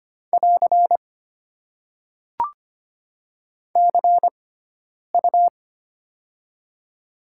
Morse code, heard as a keyed tone, about 25 wpm, spaced as for 9 wpm, 690 Hz. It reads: <AR> CU
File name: Top 90 QSO Elements - 6x Character Spacing - 25wpm